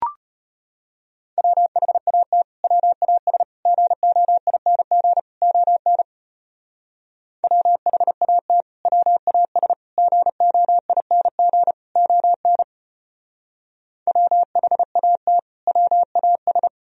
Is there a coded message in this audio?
Yes: Morse code from 1.4 to 16.7 s, heard as a keyed tone, about 32 wpm, 700 Hz. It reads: WHAT WAS GOING ON WHAT WAS GOING ON WHAT WAS